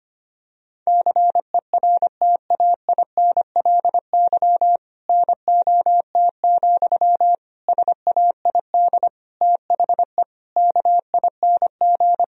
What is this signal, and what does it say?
Morse code, heard as a keyed tone, about 25 wpm, 700 Hz. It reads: CERTAINLY NOT, SAID THE KING
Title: Sentences from Top 1000 Words - Review Part 1 - 25wpm